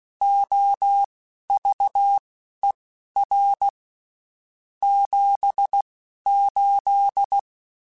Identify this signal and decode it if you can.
Morse code, heard as a keyed tone, about 16 wpm, spaced as for 11 wpm, 800 Hz. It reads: OVER 78
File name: (spaced) / QSOs Spaced-mp3s / WA7QOX 2 @15 sp